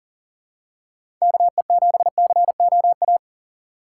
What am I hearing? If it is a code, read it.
Morse code, heard as a keyed tone, about 40 wpm, 700 Hz. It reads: KE7COA
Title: USA Call Signs - 6 Characters - 4x Character Spacing - 40wpm